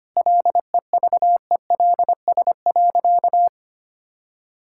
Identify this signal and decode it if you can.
Morse code, heard as a keyed tone, about 25 wpm, 700 Hz. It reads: LEVELS.
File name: practice / 25wpm / All-Age Appropriate Jokes - 25wpm